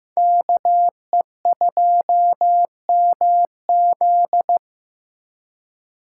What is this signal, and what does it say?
Morse code, heard as a keyed tone, about 15 wpm, 700 Hz. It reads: KE2MZ
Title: USA Call Signs - 5 Characters - 2x Character Spacing - 15wpm